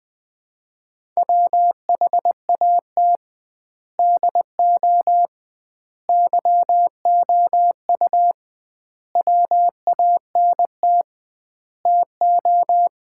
Morse code, heard as a keyed tone, about 20 wpm, 700 Hz. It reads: WHAT DO YOU WANT TO
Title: 2x Word-Spacing - Sets of 5 Words - 20wpm